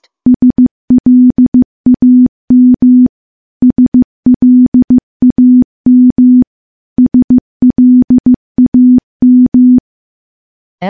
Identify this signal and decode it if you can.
Morse code, heard as a keyed tone, about 15 wpm, 250 Hz. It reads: SLAM SLAM SLAM
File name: morse-training-15WPM